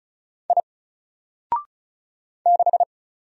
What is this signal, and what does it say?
Morse code, heard as a keyed tone, about 34 wpm, 700 Hz. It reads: I 6
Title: Single Character (Letter:Number:Punctuation:Prosign) 35wpm